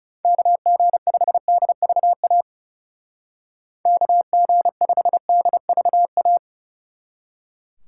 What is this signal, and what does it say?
Morse code, heard as a keyed tone, about 36 wpm, 700 Hz. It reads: KG5DVA KG5DVA
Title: Speed Racing - USA 2x3 Call Signs 30wpm